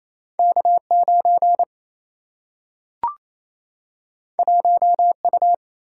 Morse code, heard as a keyed tone, about 28 wpm, 700 Hz. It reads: K9 1U
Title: Single-Letter Single-Number 28wpm